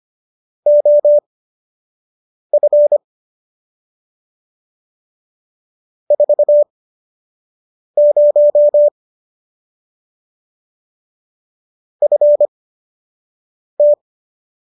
Morse code, heard as a keyed tone, about 25 wpm, spaced as for 6 wpm, 590 Hz. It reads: OF 40 FT